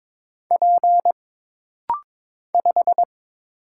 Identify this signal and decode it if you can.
Morse code, heard as a keyed tone, about 22 wpm, 700 Hz. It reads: P 5
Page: This rapid-fire timing is geared to push your ICR (Instant Character Recognition) skills.